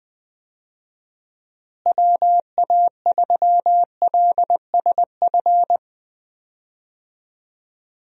Morse code, heard as a keyed tone, about 20 wpm, 700 Hz. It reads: WA3LSF
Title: USA Call Signs - 6 Characters - Repeated 2x - 20wpm